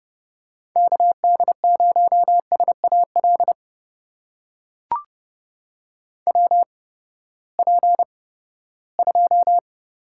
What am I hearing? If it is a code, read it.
Morse code, heard as a keyed tone, about 30 wpm, spaced as for 17 wpm, 700 Hz. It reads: KD0SAL WP2
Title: USA Call Signs - 6 Characters - 8x Character Spacing - 30wpm